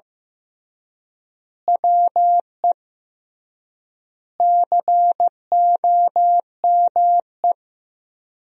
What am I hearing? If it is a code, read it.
Morse code, heard as a keyed tone, about 15 wpm, 700 Hz. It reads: WE COME
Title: 3x Word-Spacing - Sentences from Top 100 Words - 15wpm